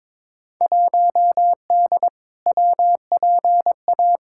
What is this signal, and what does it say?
Morse code, heard as a keyed tone, about 22 wpm, 700 Hz. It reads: 1D WPA
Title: Field Day Exchanges - 22wpm